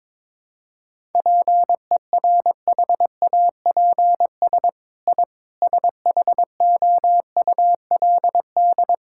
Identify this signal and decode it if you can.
Morse code, heard as a keyed tone, about 22 wpm, 700 Hz. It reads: PERHAPS I SHOULD
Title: Sentences from Top 1000 Words - Review Part 5 - 22wpm